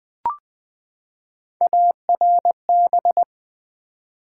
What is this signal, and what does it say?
Morse code, heard as a keyed tone, about 20 wpm, 700 Hz. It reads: ARB
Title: Build A Word - 3 Letters At A Time - 9-Letter Words - 20wpm